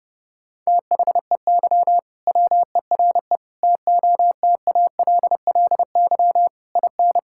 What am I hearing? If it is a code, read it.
Morse code, heard as a keyed tone, about 32 wpm, 700 Hz. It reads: THEY WERE TOTALLY IN
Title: Sentences from Top 1600 Words 30wpm